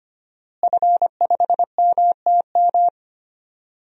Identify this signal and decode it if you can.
Morse code, heard as a keyed tone, about 25 wpm, 700 Hz. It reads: F5MTM
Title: DXCC-100 Call Signs - 5 Characters - Repeated 3x - 25wpm